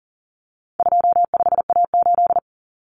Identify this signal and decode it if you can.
Morse code, heard as a keyed tone, about 40 wpm, 700 Hz. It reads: 25A8
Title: Common Cut Numbers - 4-Digit 40wpm